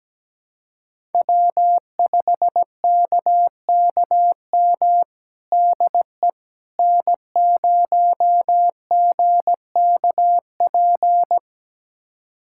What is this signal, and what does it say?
Morse code, heard as a keyed tone, about 17 wpm, 700 Hz. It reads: W5KKM DE N0GKP
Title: USA Callsign DE Callsign - 5 Characters - 17wpm